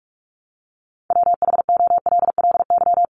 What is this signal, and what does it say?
Morse code, heard as a keyed tone, about 44 wpm, 700 Hz. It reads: WHOLLY
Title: LY Suffix - Top 100 Words with Base Words 45wpm